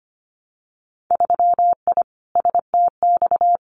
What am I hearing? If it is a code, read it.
Morse code, heard as a keyed tone, about 25 wpm, 700 Hz. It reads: 3I STX
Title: Winter Field Day Exchanges Encore 25wpm